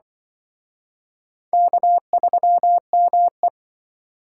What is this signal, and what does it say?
Morse code, heard as a keyed tone, about 26 wpm, 700 Hz. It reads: K3ME